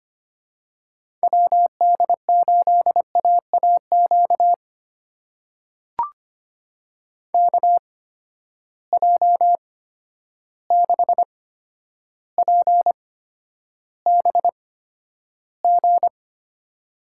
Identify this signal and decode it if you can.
Morse code, heard as a keyed tone, about 25 wpm, spaced as for 11 wpm, 700 Hz. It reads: WD8AAQ KJ6PBG
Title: USA Call Signs - 6 Characters - 8x Character Spacing - 25wpm